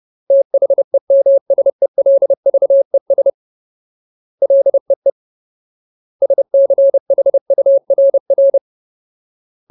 Morse code, heard as a keyed tone, about 30 wpm, 550 Hz. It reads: THEMSELVES LEE SCHURR
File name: Ultimate Word List / 30WPM Ultimate